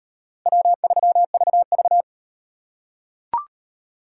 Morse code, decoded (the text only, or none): W3VV